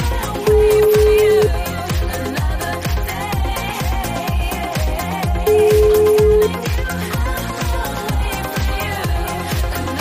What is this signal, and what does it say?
Ringback tone (Europe, 425 Hz), with noise.